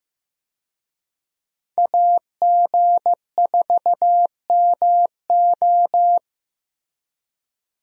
Morse code, heard as a keyed tone, about 15 wpm, 700 Hz. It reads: AG4MO
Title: USA Call Signs - 5 Characters - 15wpm